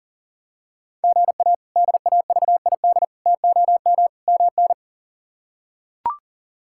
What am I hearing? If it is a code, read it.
Morse code, heard as a keyed tone, about 40 wpm, 700 Hz. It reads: GA DAVID TOM MN
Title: K1USN SST QSO Practice - 40wpm